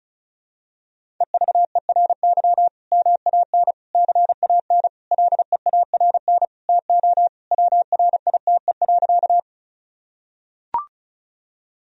Morse code, heard as a keyed tone, about 36 wpm, 700 Hz. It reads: EVERY MAN CAN LEARN TO WRITE.
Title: Sentences from Top 1000 Words - Review Part 1 - 35wpm